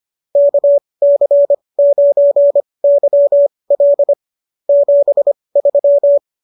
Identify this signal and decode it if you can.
Morse code, heard as a keyed tone, about 25 wpm, spaced as for 20 wpm, 590 Hz. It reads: KC9YL 73